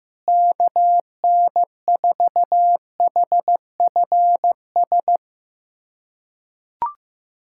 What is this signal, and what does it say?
Morse code, heard as a keyed tone, about 15 wpm, 700 Hz. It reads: KN4HFS
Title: USA Callsign DE Callsign 15wpm